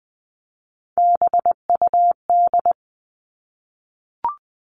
Morse code, heard as a keyed tone, about 20 wpm, 700 Hz. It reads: BUD
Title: All Cut Numbers - 3-Digit 20wpm